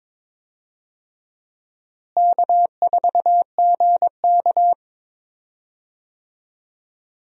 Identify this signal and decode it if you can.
Morse code, heard as a keyed tone, about 22 wpm, 700 Hz. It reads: K4GK
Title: USA Call Signs - 4 Characters - Repeated 3x - 22wpm